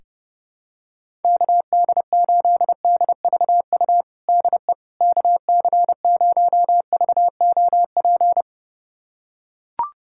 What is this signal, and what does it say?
Morse code, heard as a keyed tone, about 30 wpm, 700 Hz. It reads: KD8DVU DE KC0VOP
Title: USA Callsign DE Callsign 30wpm